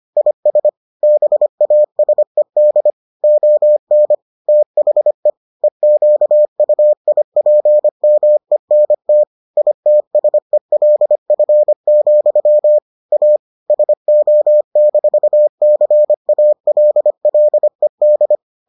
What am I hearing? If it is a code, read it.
Morse code, heard as a keyed tone, about 25 wpm, 600 Hz. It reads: IS BASED ON THE EQUIPMENT ITSELF, A SO-CALLED